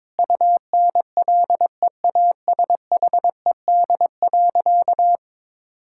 Morse code, heard as a keyed tone, about 21 wpm, 700 Hz. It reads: UNLEASHED.
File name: Difficult Sentences with Hints - 22wpm